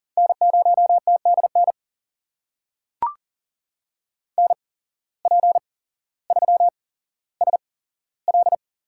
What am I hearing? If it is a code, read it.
Morse code, heard as a keyed tone, about 40 wpm, spaced as for 17 wpm, 700 Hz. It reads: N0TDN NP3SL
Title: USA Call Signs - 5 Characters - 8x Character Spacing - 40wpm